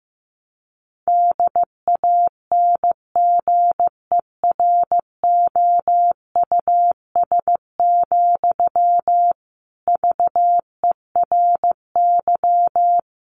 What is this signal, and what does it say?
Morse code, heard as a keyed tone, about 15 wpm, 700 Hz. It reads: DANGEROUS, VERY
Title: Sentences from Top 1200 Words 15wpm